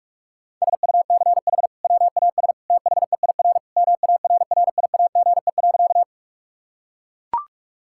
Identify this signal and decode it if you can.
Morse code, heard as a keyed tone, about 41 wpm, 690 Hz. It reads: SUCH WAS THEIR MARRIAGE.
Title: Sentences from Top 1000 Words - Review Part 5 - 45wpm